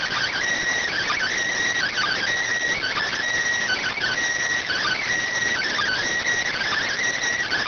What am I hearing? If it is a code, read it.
Emergency Alert System (SAME) tones (header bursts), heard as harsh buzzing bursts. It is heard from 0.8 to 7.3 s.